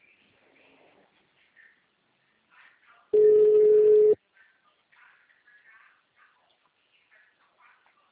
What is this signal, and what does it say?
Ringback tone (Europe, 425 Hz).
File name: out- -2040- - - .166